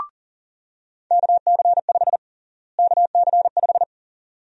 Morse code, heard as a keyed tone, about 40 wpm, 700 Hz. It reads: KC5 KC5
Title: USA Call Sign Prefixes - Repeated 2x - 40wpm